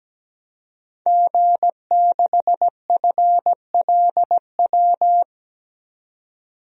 Morse code, heard as a keyed tone, about 17 wpm, 700 Hz. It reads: G6FLW
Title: DXCC-100 Call Signs - 5 Characters - 17wpm